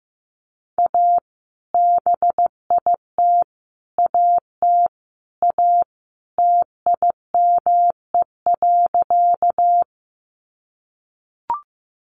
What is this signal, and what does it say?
Morse code, heard as a keyed tone, about 15 wpm, 700 Hz. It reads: A BIT AT A TIME.
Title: Sentences from Top 900 Words 15wpm